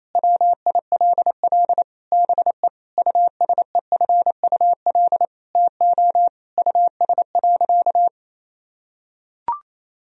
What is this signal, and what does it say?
Morse code, heard as a keyed tone, about 28 wpm, 700 Hz. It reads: WILL BE USEFUL TO US.
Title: Sentences from Top 1000 Words - Review Part 4 - 28wpm